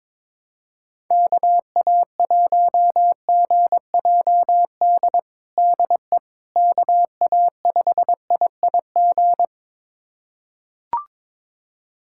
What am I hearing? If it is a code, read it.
Morse code, heard as a keyed tone, about 22 wpm, 700 Hz. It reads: KA1GJD DE KA5IIG
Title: USA Callsign DE Callsign - 6 Characters - 22wpm